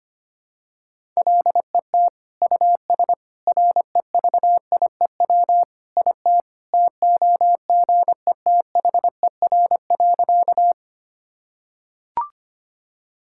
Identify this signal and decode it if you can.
Morse code, heard as a keyed tone, about 25 wpm, 700 Hz. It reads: LET US REVIEW IT TOGETHER.
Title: Sentences from Top 1000 Words - Review Part 6 - 25wpm